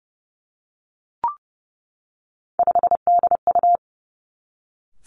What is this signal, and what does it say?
Morse code, heard as a keyed tone, about 32 wpm, 700 Hz. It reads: 5DU
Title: Three Challenging Letter-Number Combinations 30wpm